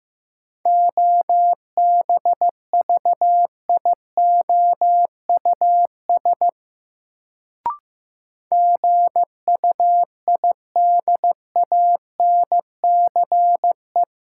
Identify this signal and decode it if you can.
Morse code, heard as a keyed tone, about 15 wpm, 700 Hz. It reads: OBVIOUS GUIDANCE
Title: Sentences from Top 1600 Words - New Word Drill - 15wpm